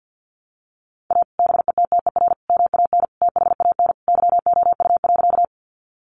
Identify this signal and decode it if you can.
Morse code, heard as a keyed tone, about 50 wpm, 700 Hz. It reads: A BETTER MAN THAN YOU.
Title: Sentences from Top 400 Words 50wpm